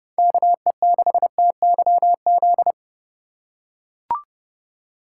Morse code, heard as a keyed tone, about 30 wpm, 700 Hz. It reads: KE6TYZ